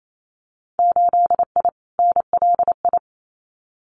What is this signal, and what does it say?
Morse code, heard as a keyed tone, about 28 wpm, 690 Hz. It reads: 8I NLI
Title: Winter Field Day Exchanges Encore 28wpm